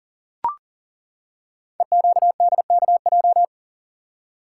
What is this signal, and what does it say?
Morse code, heard as a keyed tone, about 38 wpm, 700 Hz. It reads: EQDKJ
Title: Code Groups - 5 Letters - 40wpm